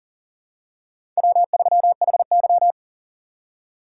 Morse code, heard as a keyed tone, about 40 wpm, 690 Hz. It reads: W3HY
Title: USA Call Signs - 6x Character Spacing - 40wpm